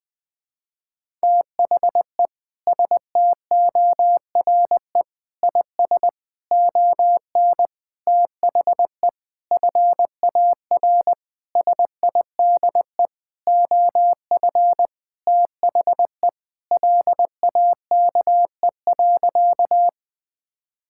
Morse code, heard as a keyed tone, about 20 wpm, 700 Hz. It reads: THE STORE IS ON THE FAR SIDE OF THE LAKE.
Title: Sentences with 6-Word Phrases - Spoken Beginning - 20wpm